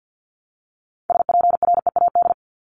Morse code, heard as a keyed tone, about 50 wpm, 700 Hz. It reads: SPREAD
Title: Top 1001-1250 Words 50wpm